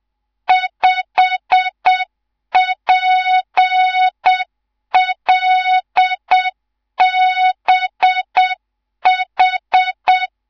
Morse code, heard as a keyed tone, about 7 wpm, 3700 Hz, with noise. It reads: 5PLBH
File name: Morze-HA2MN/hang/11 GYAKORLAT